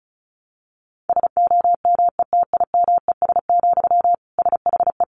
Morse code, heard as a keyed tone, about 34 wpm, 700 Hz. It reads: SOMETIMES, SHE
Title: Sentences from Top 1000 Words 35wpm